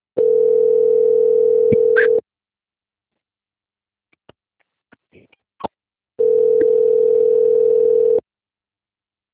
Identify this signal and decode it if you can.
Ringback tone (North America), 440 + 480 Hz.
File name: out- - - - - .7495